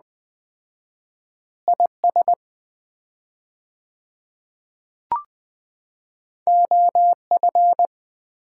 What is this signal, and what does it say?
Morse code, heard as a keyed tone, about 20 wpm, 700 Hz. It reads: IS OF
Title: 4x Word-Spacing - Sets of 2 Words - 20wpm